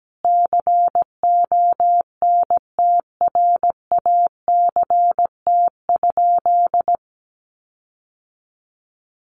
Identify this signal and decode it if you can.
Morse code, heard as a keyed tone, about 17 wpm, 700 Hz. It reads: CONTRACT?
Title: Sentences from Top 1400 Words 17wpm